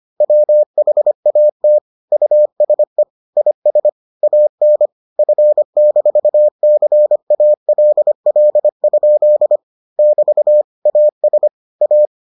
Morse code, heard as a keyed tone, about 25 wpm, 600 Hz. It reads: WHAT USE IS AN F-CALL? <BT> AS A